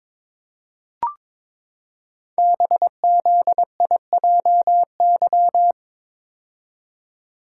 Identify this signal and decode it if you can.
Morse code, heard as a keyed tone, about 22 wpm, 700 Hz. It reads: BZIJY
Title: Code Groups - 5 Letters - 22wpm